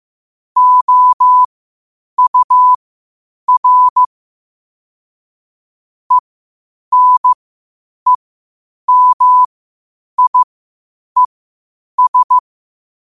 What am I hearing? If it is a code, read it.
Morse code, heard as a keyed tone, about 15 wpm, spaced as for 8 wpm, 1000 Hz. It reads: OUR ENEMIES
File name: qotdfiles / 8wpm